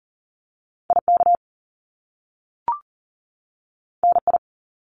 Morse code, heard as a keyed tone, about 42 wpm, 700 Hz. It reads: IK NI
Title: Top Two Letter Combinations in Words 40wpm